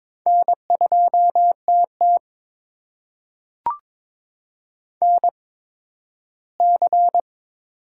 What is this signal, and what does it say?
Morse code, heard as a keyed tone, about 22 wpm, spaced as for 13 wpm, 700 Hz. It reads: N2TT NC
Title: USA Call Signs - 4 Characters - 8x Character Spacing - 22wpm